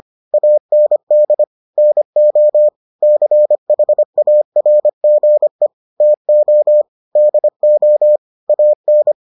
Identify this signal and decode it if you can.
Morse code, heard as a keyed tone, about 25 wpm, 600 Hz. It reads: AND NO CHARGE TO DO AN